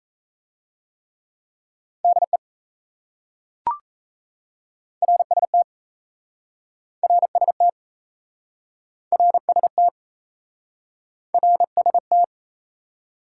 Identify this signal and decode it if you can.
Morse code, heard as a keyed tone, about 38 wpm, 700 Hz. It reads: DE RST RST RST RST